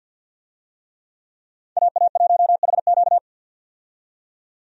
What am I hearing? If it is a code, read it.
Morse code, heard as a keyed tone, about 49 wpm, 700 Hz. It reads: AA1HQ
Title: USA Call Signs - 5 Characters - 50wpm